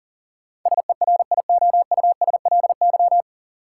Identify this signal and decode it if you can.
Morse code, heard as a keyed tone, about 40 wpm, 700 Hz. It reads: SERIOUSLY
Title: Sentences from Top 1200 Words - New Word Drill - 40wpm